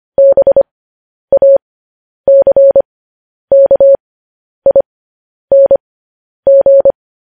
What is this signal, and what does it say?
Morse code, heard as a keyed tone, about 25 wpm, spaced as for 10 wpm, 550 Hz. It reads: BACKING
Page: news headlines in morse code at 10 wpm